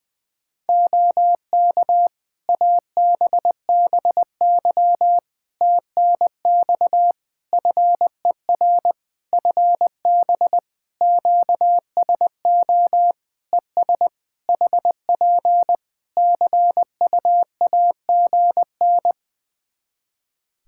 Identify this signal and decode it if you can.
Morse code, heard as a keyed tone, about 20 wpm, 700 Hz. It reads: OK ABBY TNX FER FB QSO ES HP CUAGN
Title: Practice QSO - Exchange 4 - Ending - 20wpm